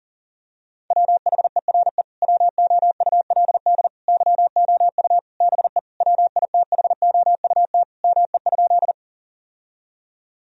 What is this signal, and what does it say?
Morse code, heard as a keyed tone, about 40 wpm, 700 Hz. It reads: WHERE WOULD YOU BE WITHOUT ME?
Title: Sentences from Top 1000 Words - Review Part 3 - 40wpm